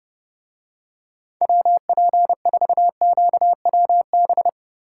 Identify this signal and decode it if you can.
Morse code, heard as a keyed tone, about 30 wpm, 700 Hz. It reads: WP4QWB